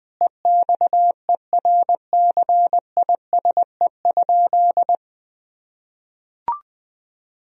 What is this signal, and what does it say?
Morse code, heard as a keyed tone, about 21 wpm, 700 Hz. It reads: EXERCISE?
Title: Sentences from Top 1000 Words - Review Part 4 - 20wpm